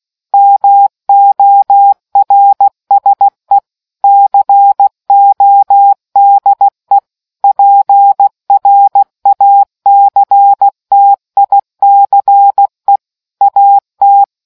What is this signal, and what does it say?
Morse code, heard as a keyed tone, about 16 wpm, 800 Hz. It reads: MORSE CODE PRACTICE AT